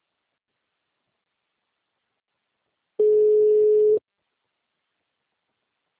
Ringback tone (Europe, 425 Hz).